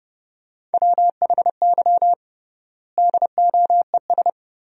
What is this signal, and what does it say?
Morse code, heard as a keyed tone, about 32 wpm, 700 Hz. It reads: WHY DOES
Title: 3x Word-Spacing - Sentences from Top 500 Words - 30wpm